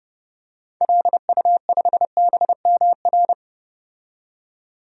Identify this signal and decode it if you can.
Morse code, heard as a keyed tone, about 30 wpm, 700 Hz. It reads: LU5BMR
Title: DXCC-100 Call Signs - 6 Characters - 30wpm